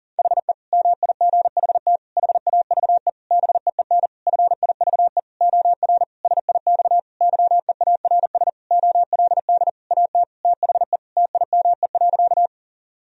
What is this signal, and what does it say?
Morse code, heard as a keyed tone, about 37 wpm, 700 Hz. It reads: HE MIGHT HAVE BEEN FIVE OR SIX YEARS OLD AT THE TIME.
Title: Sentences from Top 1000 Words - Review Part 2 - 40wpm